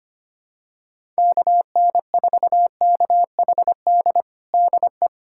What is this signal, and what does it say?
Morse code, heard as a keyed tone, about 25 wpm, 700 Hz. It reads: KN4KHD DE
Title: USA Callsign DE Callsign 25wpm